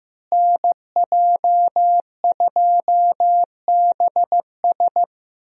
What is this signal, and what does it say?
Morse code, heard as a keyed tone, about 15 wpm, 700 Hz. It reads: NJ2BS
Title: USA Call Signs - 5 Characters - Repeated 3x - 15wpm